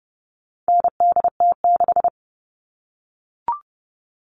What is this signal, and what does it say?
Morse code, heard as a keyed tone, about 30 wpm, 700 Hz. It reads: NDT6
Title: All Cut Numbers - 4-Digit 30wpm